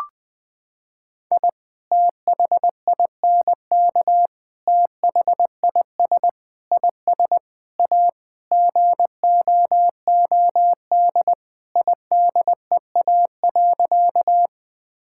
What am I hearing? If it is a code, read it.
Morse code, heard as a keyed tone, about 20 wpm, 700 Hz. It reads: I THINK THIS IS A GOOD IDEA.
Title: Sentences with 5-Word Phrases - 20wpm